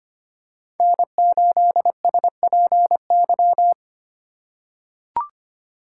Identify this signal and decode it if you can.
Morse code, heard as a keyed tone, about 25 wpm, 700 Hz. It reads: N8SPY
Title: USA Call Signs - 5 Characters - 4x Character Spacing - 25wpm